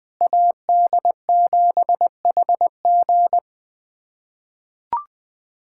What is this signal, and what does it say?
Morse code, heard as a keyed tone, about 20 wpm, 700 Hz. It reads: AD7HG
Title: USA Callsign DE Callsign - 5 Characters - 20wpm